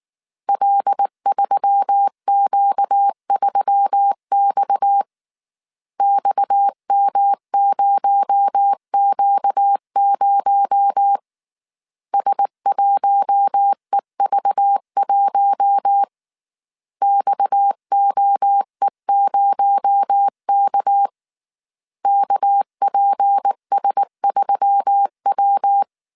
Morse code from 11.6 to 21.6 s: S1EV1 XOE0K